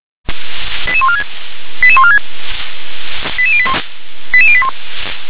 Five-tone selective call (ZVEI), heard as quick runs of five tones, tones of about 70 ms. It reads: 88146 88146 88081 88081